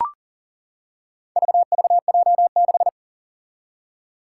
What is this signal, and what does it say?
Morse code, heard as a keyed tone, about 40 wpm, 700 Hz. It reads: VVJ6
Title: Four Challenging Letter-Number Combinations 40wpm